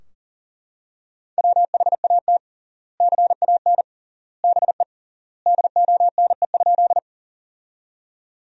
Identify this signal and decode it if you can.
Morse code, heard as a keyed tone, about 40 wpm, 700 Hz. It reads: WHAT CAN BE DONE?
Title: 3x Word-Spacing - Sentences from Top 500 Words - 40wpm